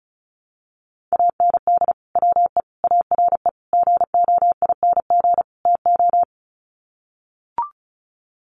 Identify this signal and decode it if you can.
Morse code, heard as a keyed tone, about 34 wpm, 700 Hz. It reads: AND WE ARE GOING TO
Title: Sets of 5 Words Encore2 35wpm